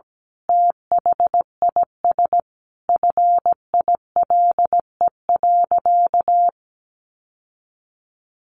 Morse code, heard as a keyed tone, about 17 wpm, 700 Hz. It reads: THIS FILE.
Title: Sentences from Top 1200 Words 17wpm